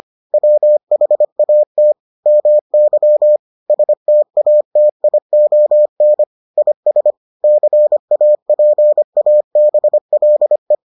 Morse code, heard as a keyed tone, about 25 wpm, 600 Hz. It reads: WHAT MY STATION IS CAPABLE